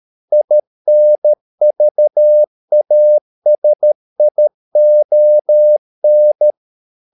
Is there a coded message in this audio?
Yes: Morse code from 0.3 to 6.5 s, heard as a keyed tone, about 13 wpm, 600 Hz. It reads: INVASION